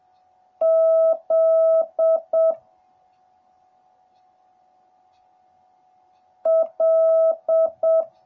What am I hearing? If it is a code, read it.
Morse code, heard as a keyed tone, about 7 wpm, 650 Hz. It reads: Z L